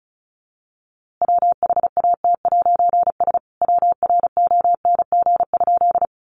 Morse code, heard as a keyed tone, about 34 wpm, 700 Hz. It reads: WHAT'S WRONG?
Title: A Sound Of Thunder 35wpm